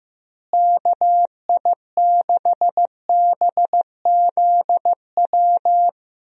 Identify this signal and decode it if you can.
Morse code, heard as a keyed tone, about 14 wpm, 700 Hz. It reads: KI6BZW
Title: USA Call Signs - 6 Characters - Repeated 3x - 15wpm